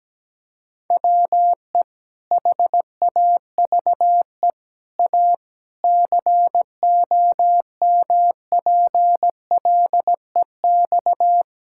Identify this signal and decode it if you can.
Morse code, heard as a keyed tone, about 17 wpm, 700 Hz. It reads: WE HAVE A COMPLEX
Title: Sentences from Top 1000 Words - Review Part 6 - 17wpm